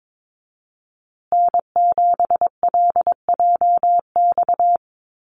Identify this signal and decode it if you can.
Morse code, heard as a keyed tone, about 22 wpm, 700 Hz. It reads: N7LJX
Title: USA Call Signs Repeated 2x 22wpm